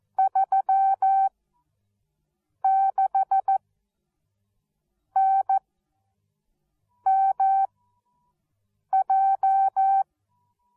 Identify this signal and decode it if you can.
Morse code, heard as a keyed tone, about 14 wpm, spaced as for 5 wpm, 770 Hz. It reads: 36NMJ
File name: Onlinelehrgang / morsen / DJ4UF Morsen13